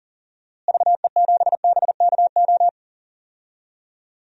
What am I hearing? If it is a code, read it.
Morse code, heard as a keyed tone, about 40 wpm, 700 Hz. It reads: VE7BKO
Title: DXCC-100 Call Signs - 6 Characters - Repeated 3x - 40wpm